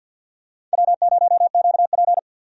Morse code, heard as a keyed tone, about 51 wpm, 700 Hz. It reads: W0QP
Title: USA Call Signs - 4 Characters - 50wpm